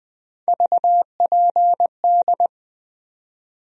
Morse code, heard as a keyed tone, about 20 wpm, 700 Hz. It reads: VPD
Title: USA Call Sign Postfixes - 3 Letters 20wpm